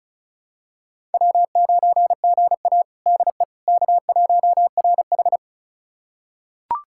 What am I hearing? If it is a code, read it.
Morse code, heard as a keyed tone, about 36 wpm, 700 Hz. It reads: W9GA DE K1RH
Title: USA Callsign DE Callsign - 4 Characters - 35wpm